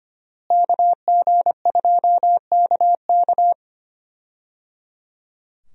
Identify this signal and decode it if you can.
Morse code, heard as a keyed tone, about 25 wpm, 700 Hz. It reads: KG2KK
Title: USA Call Signs - 5 Characters - Repeated 3x - 25wpm